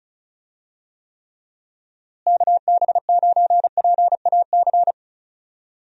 Morse code, heard as a keyed tone, about 36 wpm, 700 Hz. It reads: KB9PAC
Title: USA Call Signs - 6 Characters - Repeated 2x - 35wpm